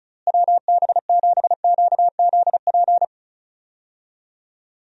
Morse code, heard as a keyed tone, about 34 wpm, 700 Hz. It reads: WB7QZP
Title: USA Call Signs - 6 Characters - Repeated 2x - 35wpm